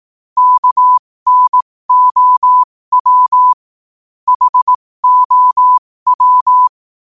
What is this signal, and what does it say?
Morse code, heard as a keyed tone, about 18 wpm, spaced as for 15 wpm, 1000 Hz. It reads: KNOW HOW
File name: qotdfiles / 15wpm